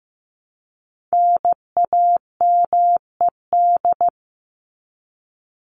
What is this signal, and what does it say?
Morse code, heard as a keyed tone, about 15 wpm, 700 Hz. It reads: NAMED
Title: ED Suffix - Top 100 Words with Base Words 15wpm